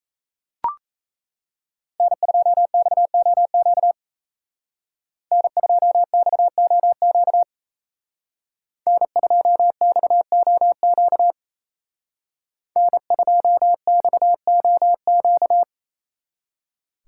Morse code, heard as a keyed tone, about 36 wpm, 700 Hz. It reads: N2XOQ N2XOQ N2XOQ N2XOQ